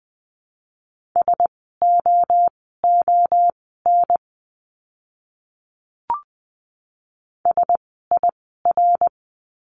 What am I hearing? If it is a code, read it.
Morse code, heard as a keyed tone, about 20 wpm, spaced as for 14 wpm, 700 Hz. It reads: SOON SIR
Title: 054 Lesson 10f - Letter r - Words 20wpm